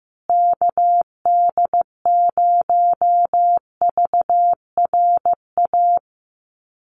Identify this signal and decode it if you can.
Morse code, heard as a keyed tone, about 15 wpm, 700 Hz. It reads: KD0VRA
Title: USA Call Signs Repeated 2x 15wpm